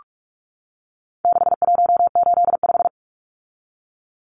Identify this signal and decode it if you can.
Morse code, heard as a keyed tone, about 44 wpm, 700 Hz. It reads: 6185
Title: Numbers - 4-Digit 45wpm